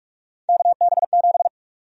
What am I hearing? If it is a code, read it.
Morse code, heard as a keyed tone, about 46 wpm, 700 Hz. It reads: KB7
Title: USA Call Sign Prefixes - Repeated 2x - 45wpm